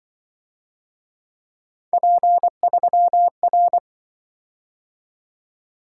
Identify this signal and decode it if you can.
Morse code, heard as a keyed tone, about 24 wpm, 700 Hz. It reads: P3R
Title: Speed-Racing - Three Letter-Number Combinations - 20wpm